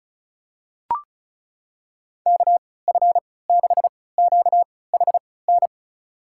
Morse code, heard as a keyed tone, about 36 wpm, spaced as for 20 wpm, 700 Hz. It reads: KF6QHN
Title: USA Call Signs - 6 Characters - 3x Character Spacing - 35wpm